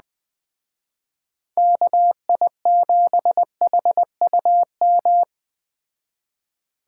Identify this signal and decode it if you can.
Morse code, heard as a keyed tone, about 19 wpm, 700 Hz. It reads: KI7HUM
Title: Speed Racing - USA 2x3 Call Signs 17wpm